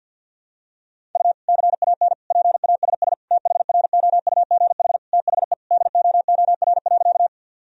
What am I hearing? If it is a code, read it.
Morse code, heard as a keyed tone, about 52 wpm, 700 Hz. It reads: U CAN PASS THROUGH THE DOOR.
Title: Sentences from Top 400 Words and Abbreviations 50wpm